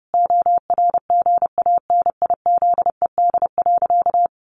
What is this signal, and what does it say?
Morse code, heard as a keyed tone, about 30 wpm, 700 Hz. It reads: ORGANIZED.
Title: Sentences from Top 1300 Words 30wpm